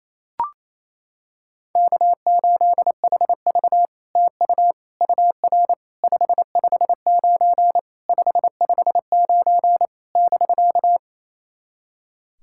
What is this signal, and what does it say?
Morse code, heard as a keyed tone, about 28 wpm, 700 Hz. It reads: K8HV TU UR 559 559 <BK>
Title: SOTA QSO Practice 28wpm